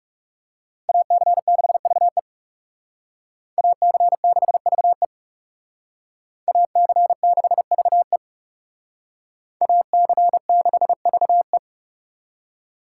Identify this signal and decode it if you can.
Morse code, heard as a keyed tone, about 38 wpm, 700 Hz. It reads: AC6VE AC6VE AC6VE AC6VE